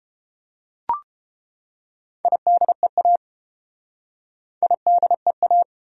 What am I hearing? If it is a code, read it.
Morse code, heard as a keyed tone, about 29 wpm, 700 Hz. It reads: IDEA IDEA